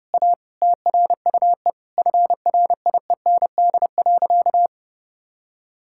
Morse code, heard as a keyed tone, about 30 wpm, 700 Hz. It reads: A TRUE FRIEND.
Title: Sentences from Top 1000 Words - Review Part 6 - 30wpm